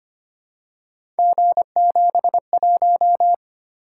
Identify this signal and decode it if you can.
Morse code, heard as a keyed tone, about 25 wpm, 700 Hz. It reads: G71
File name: Three Letter-Number Combinations - 25wpm